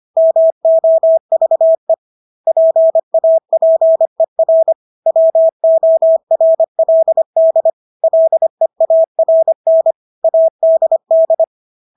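Morse code, heard as a keyed tone, about 25 wpm, 650 Hz. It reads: MOVE PAPER WORLD LEARN ADD
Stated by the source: Morse practice of English words at 25 wpm from CW med Gnister.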